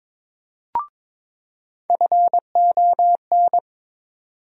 Morse code, heard as a keyed tone, about 22 wpm, 700 Hz. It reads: FON
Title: USA Call Sign Postfixes - 3 Letters 22wpm